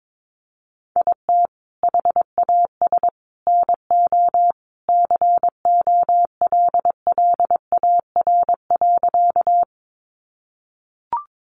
Morse code, heard as a keyed tone, about 22 wpm, 700 Hz. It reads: IT HAS NO COLLAR.
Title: Easy Phrases 22wpm